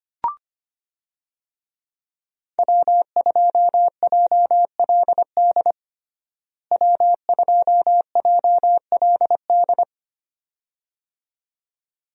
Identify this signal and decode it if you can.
Morse code, heard as a keyed tone, about 25 wpm, 700 Hz. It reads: W2JLD W2JLD
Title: USA Call Signs - 5 Characters - Repeated 2x - 25wpm